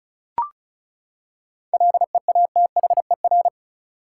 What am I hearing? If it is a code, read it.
Morse code, heard as a keyed tone, about 34 wpm, 700 Hz. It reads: LEATHER
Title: Seven Letter Words - 35wpm